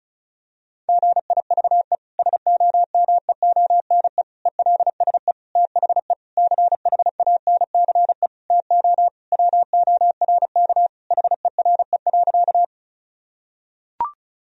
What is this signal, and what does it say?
Morse code, heard as a keyed tone, about 34 wpm, 700 Hz. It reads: GIVE SOMEONE ELSE THE CHANCE TO WORK HERE.
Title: Sentences from Top 1000 Words - Review Part 4 - 35wpm